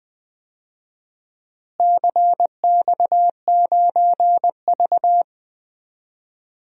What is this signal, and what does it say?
Morse code, heard as a keyed tone, about 20 wpm, 700 Hz. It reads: CX9V
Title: DXCC-100 Call Signs - 4 Characters - Repeated 2x - 20wpm